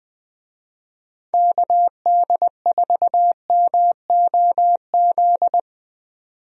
Morse code, heard as a keyed tone, about 20 wpm, 700 Hz. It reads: KD4MOZ
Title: USA Call Signs - 6 Characters - 20wpm